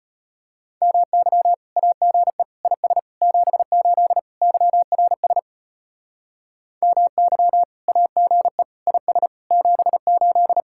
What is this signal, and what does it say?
Morse code, heard as a keyed tone, about 31 wpm, 700 Hz. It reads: MY AGE IS 78 YRS MY AGE IS 78
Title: Speed-Racing Common QSO Phrases - 25wpm